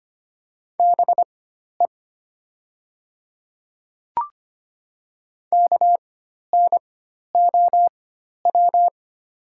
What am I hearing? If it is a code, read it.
Morse code, heard as a keyed tone, about 25 wpm, spaced as for 12 wpm, 700 Hz. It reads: BE KNOW
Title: 4x Character Spacing - Top 100 Words - 25wpm